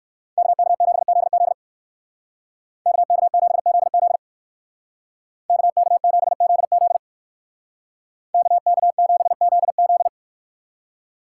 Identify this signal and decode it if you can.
Morse code, heard as a keyed tone, about 62 wpm, 700 Hz. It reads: KK7ZZ KK7ZZ KK7ZZ KK7ZZ